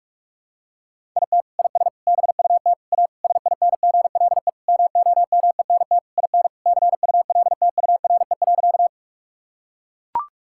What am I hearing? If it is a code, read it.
Morse code, heard as a keyed tone, about 41 wpm, 700 Hz. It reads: IT IS BUT A SINGLE MOMENT IN CULTURE.
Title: Sentences from Top 1000 Words - Review Part 6 - 45wpm